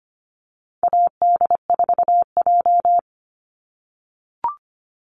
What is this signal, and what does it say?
Morse code, heard as a keyed tone, about 25 wpm, 700 Hz. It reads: AD4J
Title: Call Signs - 25wpm